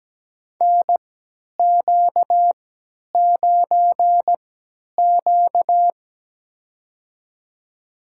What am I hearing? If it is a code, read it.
Morse code, heard as a keyed tone, about 17 wpm, spaced as for 10 wpm, 700 Hz. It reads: NQ9Q
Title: USA Call Signs - 4 Characters - 3x Character Spacing - 17wpm